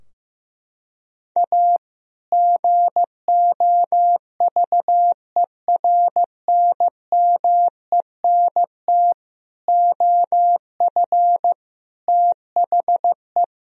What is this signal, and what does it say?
Morse code, heard as a keyed tone, about 15 wpm, 700 Hz. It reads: A GOVERNMENT OF THE
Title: Sentences from Top 1000 Words - Review Part 3 - 15wpm